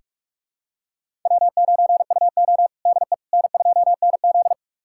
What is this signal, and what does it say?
Morse code, heard as a keyed tone, about 46 wpm, 700 Hz. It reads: W9UO DE N2NZ
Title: USA Callsign DE Callsign - 4 Characters - 45wpm